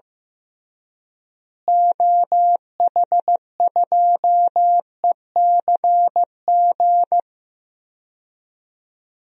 Morse code, heard as a keyed tone, about 15 wpm, 700 Hz. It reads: OH2ECG
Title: DXCC-100 Call Signs - 6 Characters - Repeated 3x - 15wpm